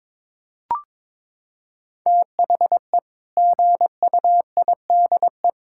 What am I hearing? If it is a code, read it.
Morse code, heard as a keyed tone, about 22 wpm, 700 Hz. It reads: THE GUIDE
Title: Sentences from Top 1000 Words - Review Part 6 - 22wpm